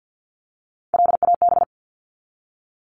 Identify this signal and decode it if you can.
Morse code, heard as a keyed tone, about 48 wpm, 700 Hz. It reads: LAB
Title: Top 1751-2000 Words 50wpm